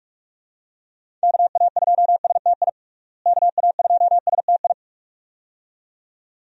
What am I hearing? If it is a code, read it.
Morse code, heard as a keyed tone, about 41 wpm, 700 Hz. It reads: KA2STI KA2STI
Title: USA Call Signs - 6 Characters - Repeated 2x - 45wpm